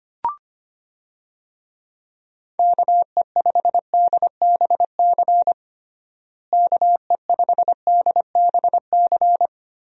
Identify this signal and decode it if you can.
Morse code, heard as a keyed tone, about 25 wpm, 700 Hz. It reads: KE5DBC KE5DBC
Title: USA Call Signs - 6 Characters - Repeated 2x - 25wpm